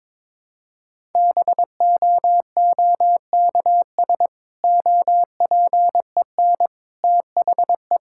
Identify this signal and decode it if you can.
Morse code, heard as a keyed tone, about 22 wpm, 700 Hz. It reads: BOOKS OPEN THE
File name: Difficult Sentences with Hints - 22wpm